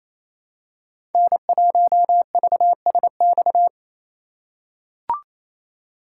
Morse code, heard as a keyed tone, about 28 wpm, 700 Hz. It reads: N1VSX
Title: USA Call Signs - 5 Characters - 4x Character Spacing - 28wpm